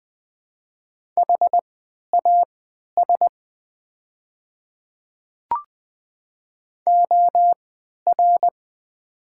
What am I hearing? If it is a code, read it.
Morse code, heard as a keyed tone, about 20 wpm, spaced as for 11 wpm, 700 Hz. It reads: HAS OR
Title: 3x Character Spacing - Top 300 Words - 20wpm